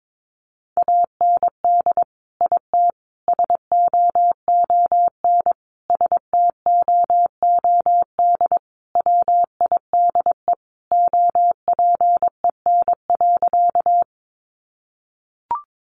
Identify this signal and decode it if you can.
Morse code, heard as a keyed tone, about 22 wpm, 700 Hz. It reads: AND IT SOON STOOD WIDE OPEN.
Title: Sentences from Top 900 Words 22wpm